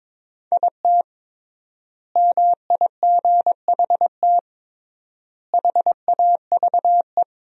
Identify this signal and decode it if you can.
Morse code, heard as a keyed tone, about 22 wpm, 700 Hz. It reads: IT MIGHT HAVE
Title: 3x Word-Spacing - Sentences from Top 500 Words - 22wpm